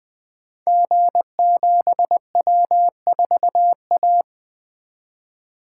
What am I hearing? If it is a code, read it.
Morse code, heard as a keyed tone, about 20 wpm, 700 Hz. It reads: G7W4A
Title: Code Groups - 5 Letter:Numbers - 20wpm